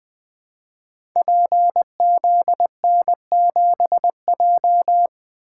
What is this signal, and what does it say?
Morse code, heard as a keyed tone, about 20 wpm, 700 Hz. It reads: PZN7J
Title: Code Groups - 5 Letter:Numbers - 20wpm